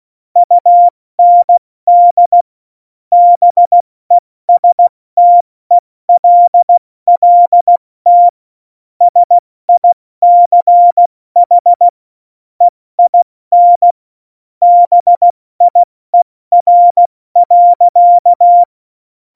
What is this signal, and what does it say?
Morse code, heard as a keyed tone, about 16 wpm, spaced as for 14 wpm, 700 Hz. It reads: UND BESTELLT SICH EIN BIER.
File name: Morsekurs / Witze / Witz 15. Oktober 2019, 70 F80